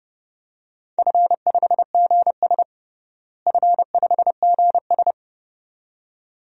Morse code, heard as a keyed tone, about 29 wpm, 700 Hz. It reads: F5GS F5GS
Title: DXCC-100 Call Signs - 4 Characters - Repeated 2x - 30wpm